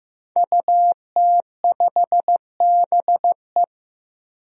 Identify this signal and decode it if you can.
Morse code, heard as a keyed tone, about 16 wpm, 700 Hz. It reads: UT5BE
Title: DXCC-100 Call Signs - 5 Characters - Repeated 2x - 15wpm